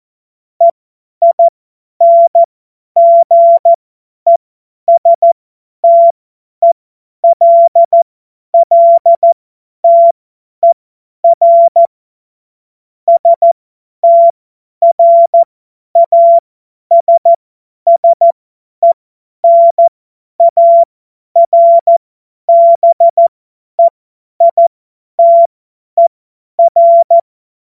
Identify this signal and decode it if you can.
Morse code, heard as a keyed tone, about 14 wpm, spaced as for 10 wpm, 680 Hz. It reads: EINGESTELLTER STRASSENARBEITER
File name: Morsekurs / Witze / Witz 24.Oktober 2023 50 F70